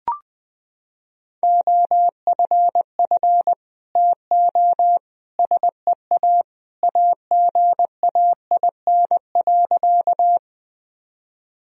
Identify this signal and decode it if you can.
Morse code, heard as a keyed tone, about 19 wpm, 700 Hz. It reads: OFF TO SEA AGAIN.
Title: Sentences from Top 1000 Words - Review Part 1 - 20wpm